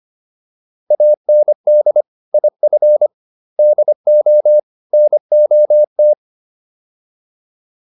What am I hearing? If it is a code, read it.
Morse code, heard as a keyed tone, about 25 wpm, 600 Hz. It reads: AND IF DO NOT